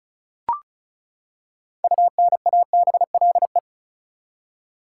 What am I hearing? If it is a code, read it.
Morse code, heard as a keyed tone, about 34 wpm, 700 Hz. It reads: UNABLE
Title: Six Letter Words - 35wpm